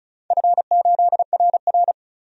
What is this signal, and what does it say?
Morse code, heard as a keyed tone, about 36 wpm, 700 Hz. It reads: F8RR
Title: DXCC-100 Call Signs - 4 Characters - 35wpm